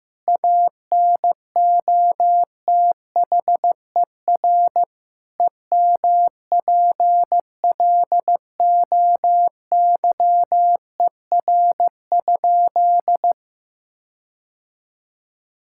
Morse code, heard as a keyed tone, about 16 wpm, 700 Hz. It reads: ANOTHER EMPLOYER?
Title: Sentences from Top 1600 Words 15wpm